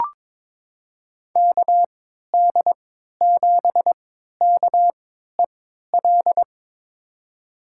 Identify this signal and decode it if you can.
Morse code, heard as a keyed tone, about 22 wpm, spaced as for 13 wpm, 700 Hz. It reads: KD7KEL
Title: USA Call Signs - 6 Characters - 3x Character Spacing - 22wpm